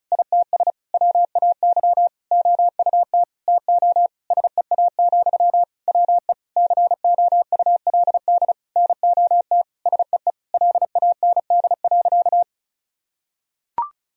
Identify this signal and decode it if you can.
Morse code, heard as a keyed tone, about 34 wpm, 700 Hz. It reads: ITS WAY OUT TO SEA, WE COULD NOT SEE LAND.
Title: Sentences from Top 1000 Words - Review Part 2 - 35wpm